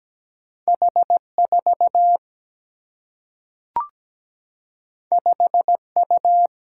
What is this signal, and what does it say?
Morse code, heard as a keyed tone, about 17 wpm, 700 Hz. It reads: H4 5U
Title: Single-Letter Single-Number 17wpm